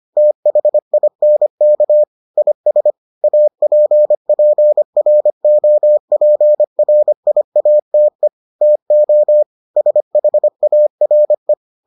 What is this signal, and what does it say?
Morse code, heard as a keyed tone, about 25 wpm, 600 Hz. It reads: THINK IS APPROPRIATE TO SHARE